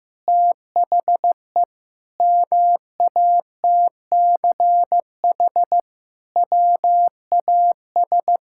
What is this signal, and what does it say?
Morse code, heard as a keyed tone, about 15 wpm, 700 Hz. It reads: THE MATCH WAS
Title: Sentences from Top 1600 Words 15wpm